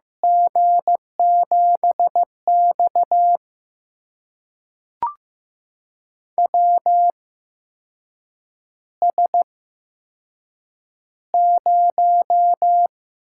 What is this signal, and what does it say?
Morse code, heard as a keyed tone, about 15 wpm, spaced as for 6 wpm, 700 Hz. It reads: G7X WS0
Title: USA Call Signs - 4 Characters - 8x Character Spacing - 15wpm